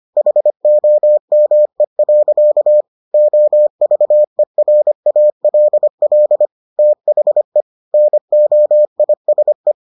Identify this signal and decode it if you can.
Morse code, heard as a keyed tone, about 25 wpm, 600 Hz. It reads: HOME. OVERALL THE NOISE